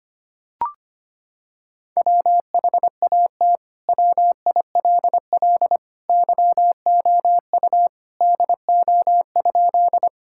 Morse code, heard as a keyed tone, about 25 wpm, 700 Hz. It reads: WHAT WILL YOU DO?
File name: practice / 25wpm / Sentences with 2-Word Phrases - 25wpm